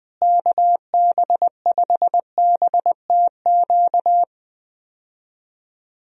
Morse code, heard as a keyed tone, about 21 wpm, 700 Hz. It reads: KB5BTQ
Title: USA Call Signs - 6 Characters - Repeated 3x - 20wpm